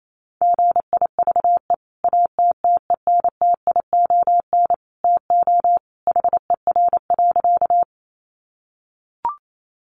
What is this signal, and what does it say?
Morse code, heard as a keyed tone, about 28 wpm, 700 Hz. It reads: GIVE ATTENTION TO HER.
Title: Sentences from Top 1000 Words 28wpm